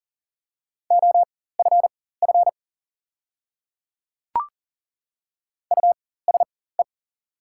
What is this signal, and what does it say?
Morse code, heard as a keyed tone, about 40 wpm, spaced as for 19 wpm, 690 Hz. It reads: OFF USE
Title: 4x Character Spacing - Top 200 Words - 40wpm